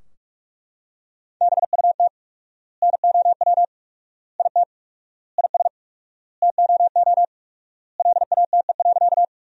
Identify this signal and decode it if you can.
Morse code, heard as a keyed tone, about 42 wpm, 700 Hz. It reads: BUT NOW IT IS TOO LATE.
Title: 4x Word-Spacing - Sentences from Top 300 Words - 45wpm